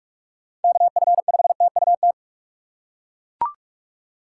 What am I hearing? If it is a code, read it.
Morse code, heard as a keyed tone, about 41 wpm, 700 Hz. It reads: KF5TUT
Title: USA Call Signs - 6 Characters - 4x Character Spacing - 45wpm